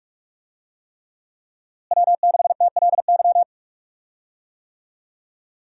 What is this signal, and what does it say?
Morse code, heard as a keyed tone, about 41 wpm, 700 Hz. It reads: W6TLY
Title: USA Call Signs - 5 Characters - Repeated 3x - 45wpm